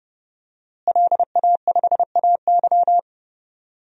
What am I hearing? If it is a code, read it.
Morse code, heard as a keyed tone, about 30 wpm, 700 Hz. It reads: LA5AY
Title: DXCC-100 Call Signs - 5 Characters - Repeated 2x - 30wpm